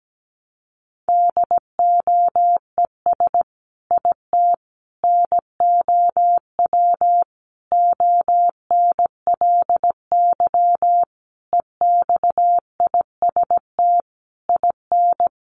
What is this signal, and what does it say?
Morse code, heard as a keyed tone, about 17 wpm, 700 Hz. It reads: DOES IT NOW ONLY EXIST IN
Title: Sentences from Top 1300 Words 17wpm